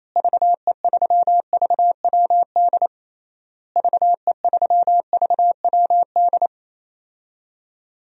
Morse code, heard as a keyed tone, about 28 wpm, 700 Hz. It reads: VE3VWD VE3VWD
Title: DXCC-100 Call Signs - 6 Characters - Repeated 3x - 28wpm